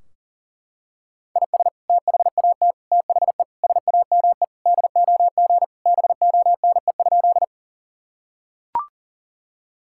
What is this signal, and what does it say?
Morse code, heard as a keyed tone, about 40 wpm, 700 Hz. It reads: IS THAT THE SAME DOG BONE?
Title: Sentences from Top 1600 Words 40wpm